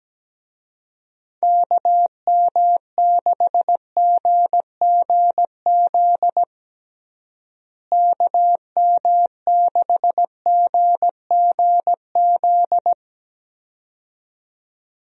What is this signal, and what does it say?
Morse code, heard as a keyed tone, about 17 wpm, 700 Hz. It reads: KM6GGZ KM6GGZ
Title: USA Call Signs - 6 Characters - Repeated 2x - 17wpm